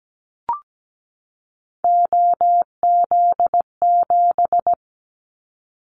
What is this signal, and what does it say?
Morse code, heard as a keyed tone, about 17 wpm, 700 Hz. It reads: OZ7
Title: Callsign Prefixes for Top 50 DXCC - 17wpm